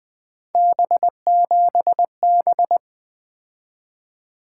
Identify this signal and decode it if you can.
Morse code, heard as a keyed tone, about 21 wpm, 700 Hz. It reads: B7B
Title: Three Letter-Number Combinations - 20wpm